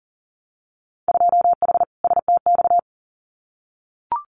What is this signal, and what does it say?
Morse code, heard as a keyed tone, about 42 wpm, 700 Hz. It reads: 2H STX
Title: Winter Field Day Exchanges Encore 40wpm